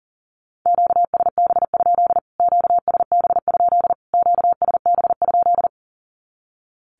Morse code, heard as a keyed tone, about 37 wpm, 700 Hz. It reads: QSB? QSB? QSB?
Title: Q-Codes Repeated 3x 40wpm